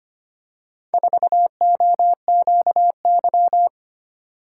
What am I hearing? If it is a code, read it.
Morse code, heard as a keyed tone, about 25 wpm, 700 Hz. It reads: 4OQY
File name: practice / 25wpm / Four Letter-Number Combinations - 25wpm